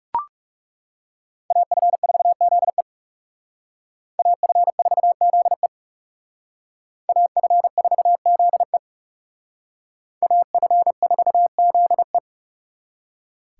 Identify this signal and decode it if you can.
Morse code, heard as a keyed tone, about 37 wpm, 700 Hz. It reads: AF4ZE AF4ZE AF4ZE AF4ZE
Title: Speed Racing - USA 2x2 Call Signs 30wpm